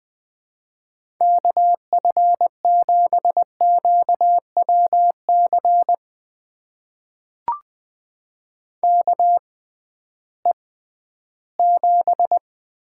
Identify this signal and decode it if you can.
Morse code, heard as a keyed tone, about 20 wpm, spaced as for 13 wpm, 700 Hz. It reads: KF7QWC KE7
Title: USA Call Signs - 6 Characters - 6x Character Spacing - 20wpm